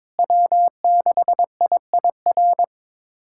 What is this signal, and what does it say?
Morse code, heard as a keyed tone, about 21 wpm, 700 Hz. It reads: W6IIR
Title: USA Callsign DE Callsign - 5 Characters - 22wpm